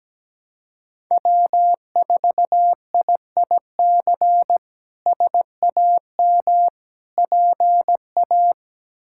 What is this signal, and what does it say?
Morse code, heard as a keyed tone, about 17 wpm, 700 Hz. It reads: W4IIC SAM PA
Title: K1USN SST QSO Practice - 17wpm